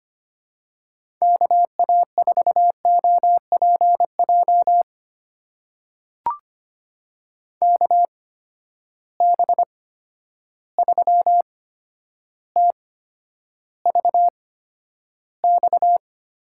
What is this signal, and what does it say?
Morse code, heard as a keyed tone, about 25 wpm, spaced as for 11 wpm, 700 Hz. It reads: KA4OPJ KB3TVX
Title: USA Call Signs - 6 Characters - 8x Character Spacing - 25wpm